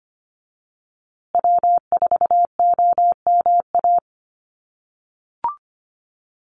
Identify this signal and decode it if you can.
Morse code, heard as a keyed tone, about 25 wpm, 700 Hz. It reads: W4OMA
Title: USA Call Signs Repeated 3x 25wpm